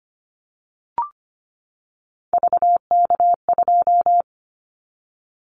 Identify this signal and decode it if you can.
Morse code, heard as a keyed tone, about 25 wpm, 700 Hz. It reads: VK2
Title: Callsign Prefixes for Top 50 DXCC - 25wpm